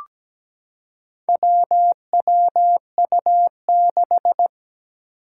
Morse code, heard as a keyed tone, about 17 wpm, 700 Hz. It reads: WWU6
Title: Four Challenging Letter-Number Combinations 17wpm